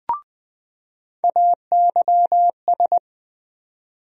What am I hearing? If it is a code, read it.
Morse code, heard as a keyed tone, about 20 wpm, 700 Hz. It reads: AYS
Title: USA Call Sign Postfixes - 3 Letters 20wpm